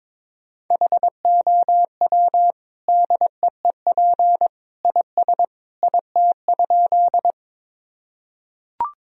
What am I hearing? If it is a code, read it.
Morse code, heard as a keyed tone, about 22 wpm, 700 Hz. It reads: HOW DEEP IS IT?
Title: Sentences from Top 1000 Words - Review Part 3 - 22wpm